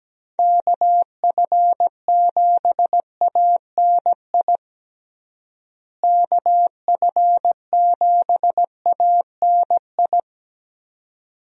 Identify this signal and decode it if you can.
Morse code, heard as a keyed tone, about 17 wpm, 700 Hz. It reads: KF7ANI KF7ANI
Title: USA Call Signs - 6 Characters - Repeated 2x - 17wpm